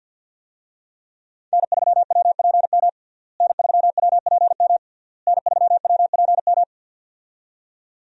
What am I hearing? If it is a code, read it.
Morse code, heard as a keyed tone, about 51 wpm, 690 Hz. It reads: N3WPM N3WPM N3WPM
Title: USA Call Signs - 5 Characters - Repeated 3x - 50wpm